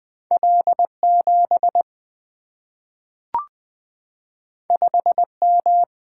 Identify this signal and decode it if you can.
Morse code, heard as a keyed tone, about 20 wpm, 700 Hz. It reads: L 7 5 M
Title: Single-Letter Single-Number 20wpm